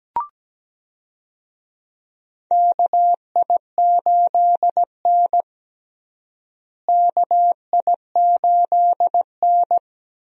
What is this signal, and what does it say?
Morse code, heard as a keyed tone, about 17 wpm, 700 Hz. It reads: KI8N KI8N
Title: USA Call Signs - 4 Characters - Repeated 2x - 17wpm